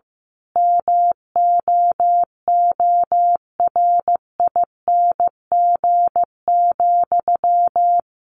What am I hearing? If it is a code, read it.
Morse code, heard as a keyed tone, about 14 wpm, 700 Hz. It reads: MOORING,
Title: A Sound Of Thunder 15wpm